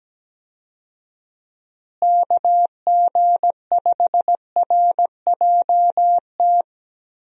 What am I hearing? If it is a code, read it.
Morse code, heard as a keyed tone, about 17 wpm, 700 Hz. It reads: KG5RJT
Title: USA Call Signs - 6 Characters - Repeated 2x - 17wpm